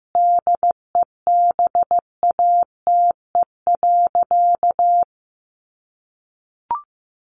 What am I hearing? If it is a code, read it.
Morse code, heard as a keyed tone, about 15 wpm, 700 Hz. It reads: DEBATE.
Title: Sentences from Top 1100 Words 15wpm